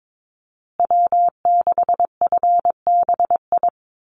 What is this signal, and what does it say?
Morse code, heard as a keyed tone, about 22 wpm, 700 Hz. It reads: W6FBI
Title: USA Call Signs Repeated 2x 22wpm